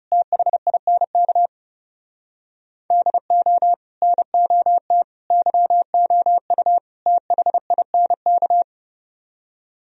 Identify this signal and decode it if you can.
Morse code, heard as a keyed tone, about 37 wpm, 700 Hz. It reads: THINK DO NOT YOU THINK
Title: Speed-Racing - Sets of 4 Words 30wpm